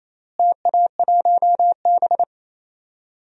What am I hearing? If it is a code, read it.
Morse code, heard as a keyed tone, about 27 wpm, 700 Hz. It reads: TA1B
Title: DXCC-100 Call Signs - 4 Characters - Repeated 3x - 28wpm